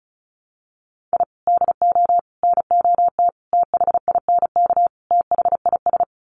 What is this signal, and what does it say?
Morse code, heard as a keyed tone, about 34 wpm, 700 Hz. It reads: I DO NOT THINK THIS
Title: Sets of 5 Words Encore 35wpm